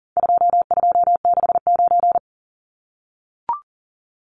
Morse code, heard as a keyed tone, about 40 wpm, 700 Hz. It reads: 2269
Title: Numbers - 4-Digit 40wpm